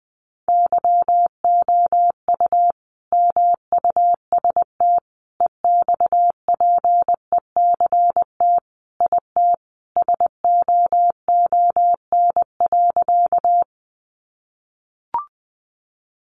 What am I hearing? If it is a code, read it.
Morse code, heard as a keyed tone, about 20 wpm, 700 Hz. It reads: YOU MUST EXPECT IT SOON.
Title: Sentences from Top 1100 Words 20wpm